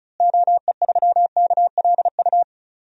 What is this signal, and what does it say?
Morse code, heard as a keyed tone, about 36 wpm, 700 Hz. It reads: OE3KLU
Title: DXCC-100 Call Signs - 6 Characters - Repeated 3x - 35wpm